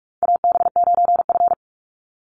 Morse code, heard as a keyed tone, about 44 wpm, 700 Hz. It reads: AB9F
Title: USA Call Signs 45wpm